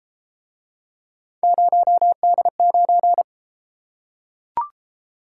Morse code, heard as a keyed tone, about 33 wpm, 700 Hz. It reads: 0D9